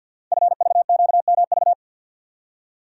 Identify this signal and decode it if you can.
Morse code, heard as a keyed tone, about 48 wpm, 700 Hz. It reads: FVQMV
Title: Code Groups - 5 Letter:Numbers - 50wpm